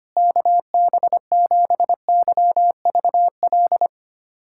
Morse code, heard as a keyed tone, about 25 wpm, 700 Hz. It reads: KB7YVL
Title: USA Callsign DE Callsign - 6 Characters - 25wpm